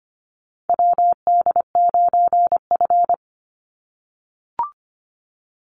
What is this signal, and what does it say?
Morse code, heard as a keyed tone, about 25 wpm, 700 Hz. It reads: WD9F
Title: NA SOTA Chaser Callsigns 25wpm